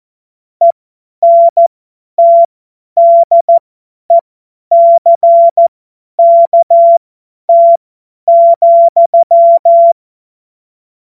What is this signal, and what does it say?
Morse code, heard as a keyed tone, about 14 wpm, spaced as for 10 wpm, 680 Hz. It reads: ENTDECKT,
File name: Morsekurs / Witze / Witz 12 Dezember 2023 50 F70